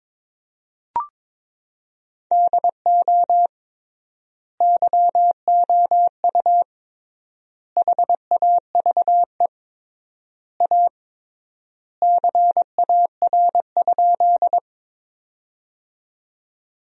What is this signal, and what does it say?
Morse code, heard as a keyed tone, about 22 wpm, 700 Hz. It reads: DO YOU HAVE A CAR?
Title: 3x Word-Spacing - Sentences from Top 300 Words - 22wpm